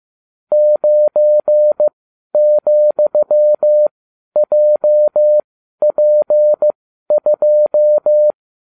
Morse code, heard as a keyed tone, about 15 wpm, spaced as for 11 wpm, 600 Hz. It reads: 9,JP2